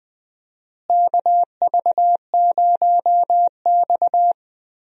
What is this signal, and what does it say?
Morse code, heard as a keyed tone, about 20 wpm, 700 Hz. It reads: KV0X
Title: USA Call Signs - 4 Characters - Repeated 2x - 20wpm